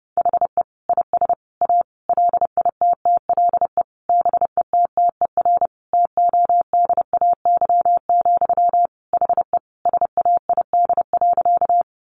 Morse code, heard as a keyed tone, about 30 wpm, 700 Hz. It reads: HE IS A LITTLE BETTER TODAY, HE SAID.
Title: Sentences from Top 600 Words 30wpm